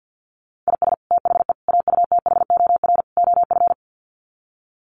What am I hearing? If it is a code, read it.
Morse code, heard as a keyed tone, about 50 wpm, 700 Hz. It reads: IS THE AUTHOR OF
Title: Sets of 4 Words 50wpm